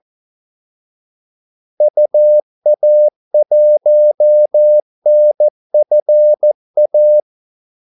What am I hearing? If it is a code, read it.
Morse code, heard as a keyed tone, about 14 wpm, 600 Hz. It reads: UA1NFA
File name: Callsigns-14wpm